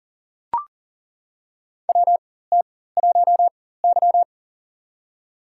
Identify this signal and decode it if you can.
Morse code, heard as a keyed tone, about 42 wpm, spaced as for 19 wpm, 690 Hz. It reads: WT1Y
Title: USA Call Signs - 4x Character Spacing - 40wpm